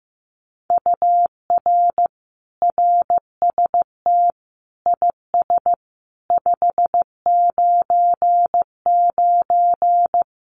Morse code, heard as a keyed tone, about 15 wpm, 700 Hz. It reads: UR RST IS 599
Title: Common QSO Phrases Encore 15wpm